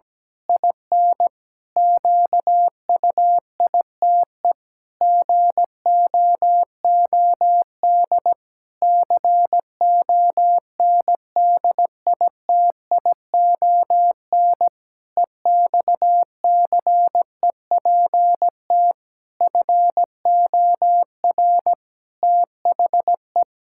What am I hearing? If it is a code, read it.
Morse code, heard as a keyed tone, about 17 wpm, 700 Hz. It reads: IN QUITE GOOD CONDITION EXCEPT FOR THE
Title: Pangram Challenge v2 - 17wpm